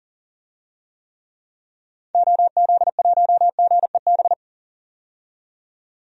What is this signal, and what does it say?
Morse code, heard as a keyed tone, about 39 wpm, 700 Hz. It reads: OZ1GEB
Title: DXCC-100 Call Signs - 6 Characters - 40wpm